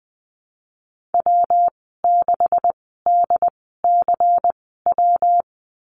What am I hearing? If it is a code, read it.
Morse code, heard as a keyed tone, about 20 wpm, spaced as for 15 wpm, 700 Hz. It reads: W6DCW
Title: 140 Lesson 23g - Letter f - Call Signs 20wpm